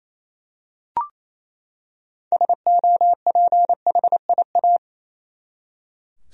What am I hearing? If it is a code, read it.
Morse code, heard as a keyed tone, about 27 wpm, 700 Hz. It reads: SOPHIA
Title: USA Common Names - 28wpm